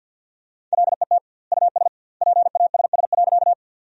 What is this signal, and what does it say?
Morse code, heard as a keyed tone, about 50 wpm, 700 Hz. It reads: LET US PASS.
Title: 2x Word-Spacing - Sentences from Top 400 Words - 50wpm